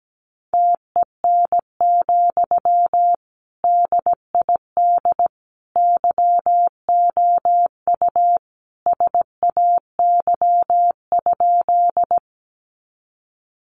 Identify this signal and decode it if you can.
Morse code, heard as a keyed tone, about 17 wpm, 700 Hz. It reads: TEN, DID YOU SAY?
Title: Sentences from Top 400 Words 17wpm